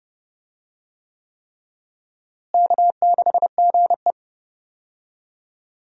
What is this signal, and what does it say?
Morse code, heard as a keyed tone, about 30 wpm, 700 Hz. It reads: K6GE
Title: USA Call Signs - 4 Characters - Repeated 3x - 30wpm